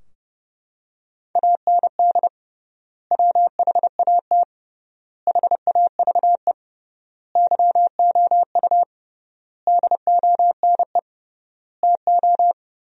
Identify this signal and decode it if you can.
Morse code, heard as a keyed tone, about 32 wpm, 700 Hz. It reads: AND WHAT HAVE YOU DONE TO
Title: 3x Word-Spacing - Sentences from Top 500 Words - 30wpm